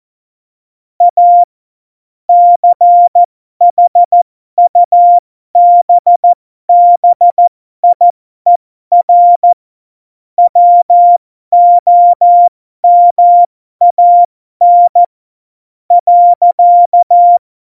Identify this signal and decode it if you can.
Morse code, heard as a keyed tone, about 14 wpm, spaced as for 12 wpm, 700 Hz. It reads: A CHUBBIER WOMAN .
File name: Morsekurs / Witze